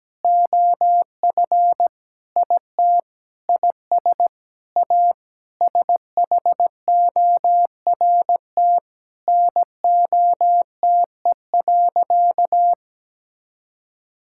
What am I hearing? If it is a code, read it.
Morse code, heard as a keyed tone, about 17 wpm, 700 Hz. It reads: OF IT IS A SHORT NOTE.
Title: Sentences from Top 1000 Words - Review Part 6 - 17wpm